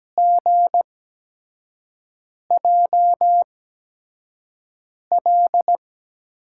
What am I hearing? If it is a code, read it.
Morse code, heard as a keyed tone, about 17 wpm, spaced as for 5 wpm, 700 Hz. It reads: GJL
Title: USA Call Signs - 5 Characters - 8x Character Spacing - 17wpm